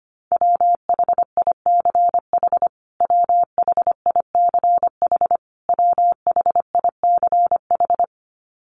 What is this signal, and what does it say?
Morse code, heard as a keyed tone, about 25 wpm, 700 Hz. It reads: WHICH WHICH WHICH
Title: Top 300 Words Repeated 25wpm